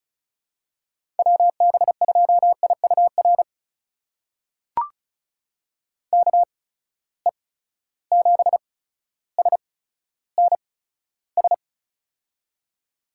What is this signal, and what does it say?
Morse code, heard as a keyed tone, about 34 wpm, spaced as for 15 wpm, 700 Hz. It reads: WB2IUR KE7SNS
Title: USA Call Signs - 6 Characters - 8x Character Spacing - 35wpm